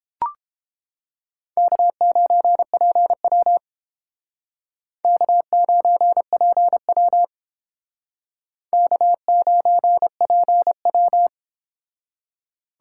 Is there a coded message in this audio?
Yes: Morse code from 1.6 to 11.3 s, heard as a keyed tone, about 31 wpm, 700 Hz. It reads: K9PW K9PW K9PW